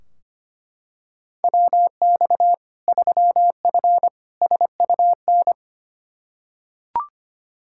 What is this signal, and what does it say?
Morse code, heard as a keyed tone, about 25 wpm, 700 Hz. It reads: WX 3F SUN
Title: Practice QSO - Exchange 2 - Info - 25wpm